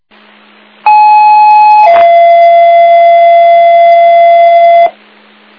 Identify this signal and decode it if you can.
Two-tone page, heard as a one-second tone, then a three-second tone, about 850 then 680 Hz, with noise. It is heard from 0.9 to 4.9 s.